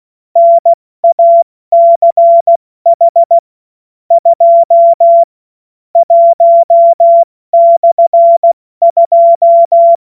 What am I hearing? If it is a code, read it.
Morse code, heard as a keyed tone, about 16 wpm, spaced as for 14 wpm, 680 Hz. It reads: NACH 2 1/2
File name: Morsekurs / Witze / Witz 02 April 2024 70 F80